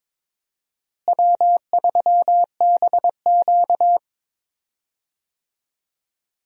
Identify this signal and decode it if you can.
Morse code, heard as a keyed tone, about 22 wpm, 700 Hz. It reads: W3BQ
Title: USA Call Signs - 4 Characters - Repeated 2x - 22wpm